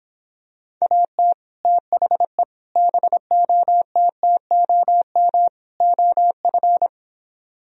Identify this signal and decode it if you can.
Morse code, heard as a keyed tone, about 26 wpm, 700 Hz. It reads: AT THE BOTTOM OF